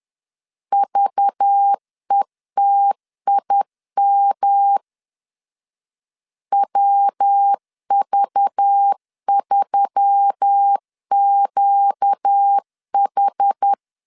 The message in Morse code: VETIM WV3QH

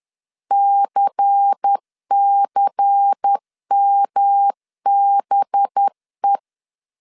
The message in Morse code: CCMBE

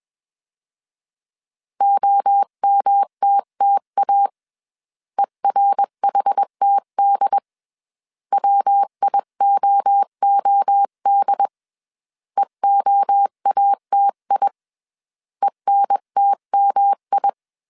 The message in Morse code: OMTTA ERHTD WIOOD EOATI ENTMI

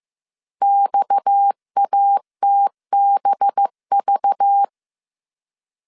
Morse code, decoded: XATBV

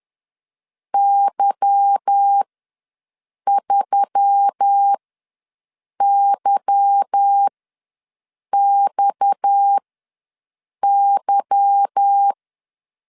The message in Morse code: Y3YXY

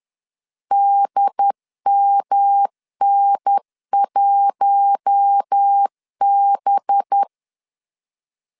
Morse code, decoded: DMN1B

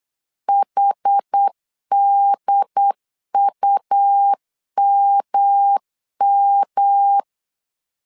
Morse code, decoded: HDUMM

